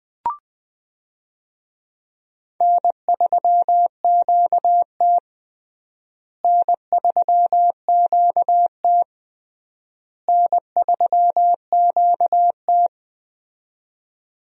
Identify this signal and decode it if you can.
Morse code, heard as a keyed tone, about 19 wpm, 700 Hz. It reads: N3QT N3QT N3QT
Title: USA Call Signs - 4 Characters - Repeated 3x - 20wpm